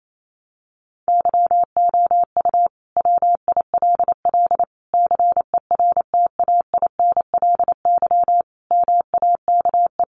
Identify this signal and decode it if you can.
Morse code, heard as a keyed tone, about 28 wpm, 700 Hz. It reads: YOU WILL CERTAINLY MAKE
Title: Sentences from Top 1400 Words 28wpm